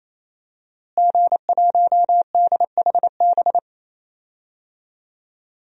Morse code, heard as a keyed tone, about 28 wpm, 700 Hz. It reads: G1DHB
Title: DXCC-100 Call Signs - 5 Characters - Repeated 3x - 28wpm